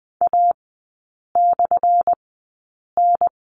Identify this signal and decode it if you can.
Morse code, heard as a keyed tone, about 20 wpm, spaced as for 8 wpm, 700 Hz. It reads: A/N
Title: 178 Lesson 29e - Slash - Three Characters 20wpm